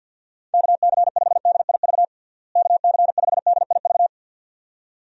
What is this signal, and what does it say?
Morse code, heard as a keyed tone, about 48 wpm, 700 Hz. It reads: KC5DIV KC5DIV
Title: USA Call Signs - 6 Characters - Repeated 2x - 50wpm